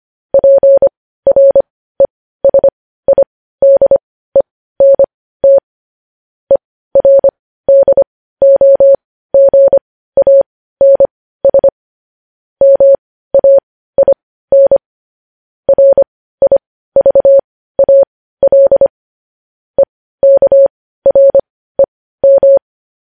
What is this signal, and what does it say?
Morse code, heard as a keyed tone, about 25 wpm, spaced as for 15 wpm, 550 Hz. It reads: PRESIDENT ERDOGANS MAIN RIVAL EKREM
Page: news-headlines-in-morse-code-at-15-wpm